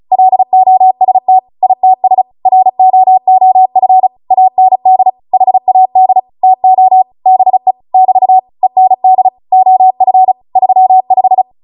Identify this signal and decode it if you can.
Morse code, heard as a keyed tone, about 36 wpm, 750 Hz. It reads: LOST ITS ROOF AND HAD TO BE <BT> END OF 35
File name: Morse / 35WPM